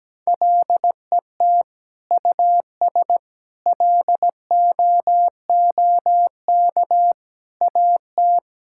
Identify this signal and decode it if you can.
Morse code, heard as a keyed tone, about 17 wpm, 700 Hz. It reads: LET US LOOK AT
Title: Sentences from Top 1000 Words - Review Part 5 - 17wpm